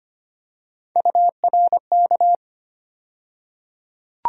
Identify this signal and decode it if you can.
Morse code, heard as a keyed tone, about 25 wpm, 700 Hz. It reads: URK
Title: USA Call Sign Postfixes - 3 Letters 25wpm